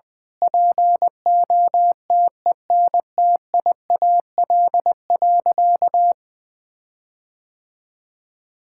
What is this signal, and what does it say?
Morse code, heard as a keyed tone, about 21 wpm, 700 Hz. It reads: POTENTIAL.
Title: Sentences with 9-Word Phrases - 20wpm